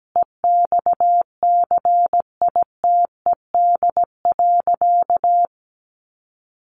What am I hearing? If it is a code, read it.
Morse code, heard as a keyed tone, about 17 wpm, 700 Hz. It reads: EXCITED.
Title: Additional Easy Phrases 17wpm